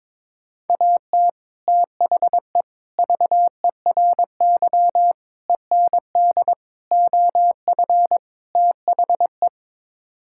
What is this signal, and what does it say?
Morse code, heard as a keyed tone, about 22 wpm, 700 Hz. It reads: AT THE VERY END OF THE
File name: practice / 22wpm / Sets of 6 Words - 22wpm